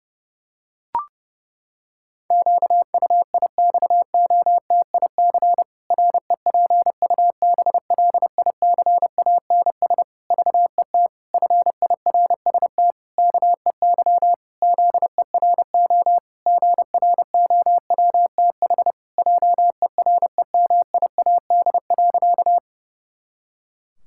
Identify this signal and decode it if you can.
Morse code, heard as a keyed tone, about 32 wpm, 700 Hz. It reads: QUIXOTIC REPUBLICANS VET FIRST KEY ZERO GROWTH JEREMIAD.
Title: Pangram Challenge v2 - 30wpm